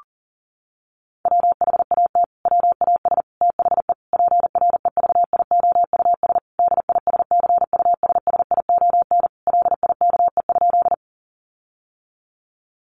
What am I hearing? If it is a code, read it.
Morse code, heard as a keyed tone, about 40 wpm, 700 Hz. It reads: WHAT WAS THE PREVIOUS DISCUSSION LIKE?
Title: Sentences from Top 1500 Words 40wpm